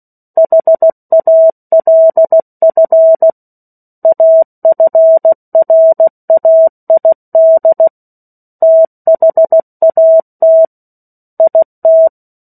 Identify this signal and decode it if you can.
Morse code, heard as a keyed tone, about 16 wpm, 650 Hz. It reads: HALF AFRAID THAT IT
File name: 03-Chapter-03 16 WPM